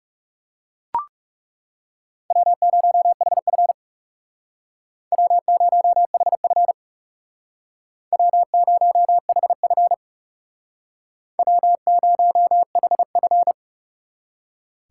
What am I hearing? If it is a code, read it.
Morse code, heard as a keyed tone, about 38 wpm, 700 Hz. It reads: W0HF W0HF W0HF W0HF